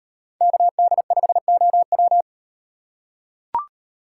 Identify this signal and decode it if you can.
Morse code, heard as a keyed tone, about 35 wpm, 700 Hz. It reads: KD5OW